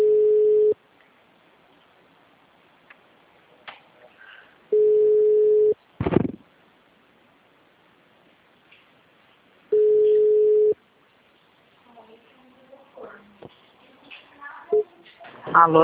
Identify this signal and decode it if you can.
Ringback tone (Europe, 425 Hz), with speech.
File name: out- -2035- - - .48839